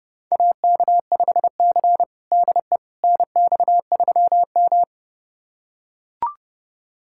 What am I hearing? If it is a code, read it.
Morse code, heard as a keyed tone, about 32 wpm, 700 Hz. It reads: AK5C DE NX3M
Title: USA Callsign DE Callsign - 4 Characters - 30wpm